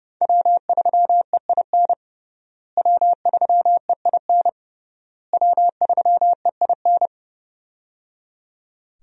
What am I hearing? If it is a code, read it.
Morse code, heard as a keyed tone, about 30 wpm, 700 Hz. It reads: W3EIN W3EIN W3EIN
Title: USA Call Signs - 5 Characters - Repeated 3x - 30wpm